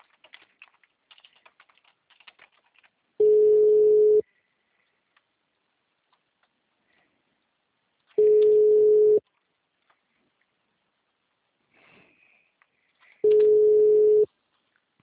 Ringback tone (Europe, 425 Hz).